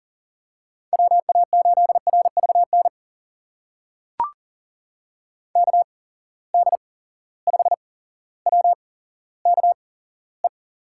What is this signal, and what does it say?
Morse code, heard as a keyed tone, about 40 wpm, spaced as for 17 wpm, 700 Hz. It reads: WA8RVN KD5WKE
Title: USA Call Signs - 6 Characters - 8x Character Spacing - 40wpm